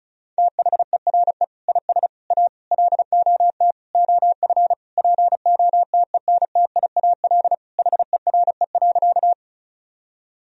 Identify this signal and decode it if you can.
Morse code, heard as a keyed tone, about 34 wpm, 700 Hz. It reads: THERE IS A LOT OF POTENTIAL HERE.
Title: Sentences from Top 1000 Words - Review Part 4 - 35wpm